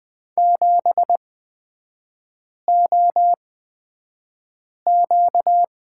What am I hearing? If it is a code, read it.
Morse code, heard as a keyed tone, about 20 wpm, spaced as for 5 wpm, 700 Hz. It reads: 7OQ
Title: 255 Lesson 41e - Period - Three Characters 20wpm